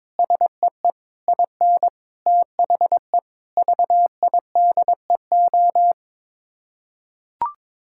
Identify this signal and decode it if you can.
Morse code, heard as a keyed tone, about 22 wpm, 700 Hz. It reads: SEE IN THE VIDEO
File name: practice / 22wpm / Sets of 7 Words - 22wpm